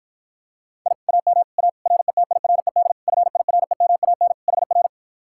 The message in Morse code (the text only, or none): I AM A RETIRED FIREMAN HR